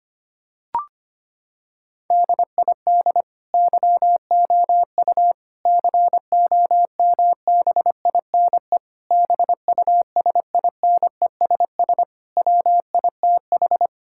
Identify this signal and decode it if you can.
Morse code, heard as a keyed tone, about 25 wpm, 700 Hz. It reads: DID YOU COMBINE BUSINESS WITH
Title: Sentences from Top 1600 Words 25wpm